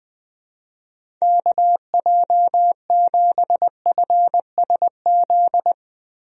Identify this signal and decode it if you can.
Morse code, heard as a keyed tone, about 20 wpm, 700 Hz. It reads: KJ7FSZ
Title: USA Call Signs - 6 Characters - Repeated 2x - 20wpm